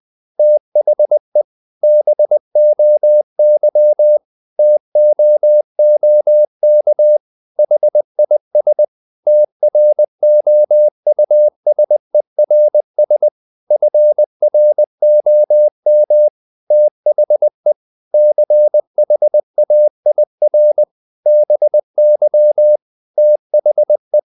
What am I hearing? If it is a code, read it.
Morse code, heard as a keyed tone, about 20 wpm, 600 Hz. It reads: THE BOY TOOK HIS TROUSERS FROM THE CHAIR BY THE
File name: the old man and the sea / the old man and the sea part 1 20wpm